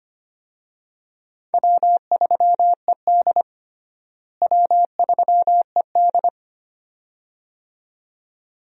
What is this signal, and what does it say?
Morse code, heard as a keyed tone, about 25 wpm, 700 Hz. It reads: W3ED W3ED
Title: USA Call Signs - 4 Characters - Repeated 2x - 25wpm